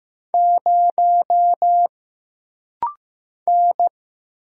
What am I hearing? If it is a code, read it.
Morse code, heard as a keyed tone, about 15 wpm, 700 Hz. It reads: 0 N
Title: Single Letter-Number Alt - Rapid-Fire 15wpm